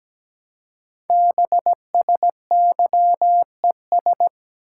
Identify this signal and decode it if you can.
Morse code, heard as a keyed tone, about 17 wpm, 700 Hz. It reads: BSYES
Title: Code Groups - 5 Letters - 17wpm